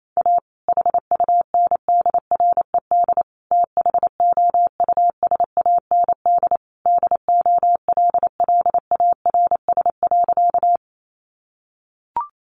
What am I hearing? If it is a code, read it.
Morse code, heard as a keyed tone, about 28 wpm, 700 Hz. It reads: A HUNDRED THOUSAND DOLLARS.
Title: A Sound Of Thunder 28wpm